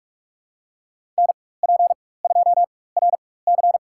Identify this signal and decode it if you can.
Morse code, heard as a keyed tone, about 44 wpm, spaced as for 21 wpm, 700 Hz. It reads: NP2RC
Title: USA Call Signs - 4x Character Spacing - 45wpm